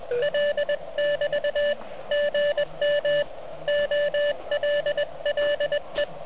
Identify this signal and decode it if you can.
Morse code, heard as a keyed tone, about 19 wpm, 590 Hz, with noise. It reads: L<BT> GM OLLE
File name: ARCHIV / OK QRP 2021 / REC12